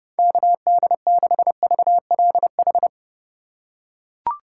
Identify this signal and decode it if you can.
Morse code, heard as a keyed tone, about 31 wpm, 700 Hz. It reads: KD6VLH
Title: USA Call Signs - 6 Characters - 4x Character Spacing - 30wpm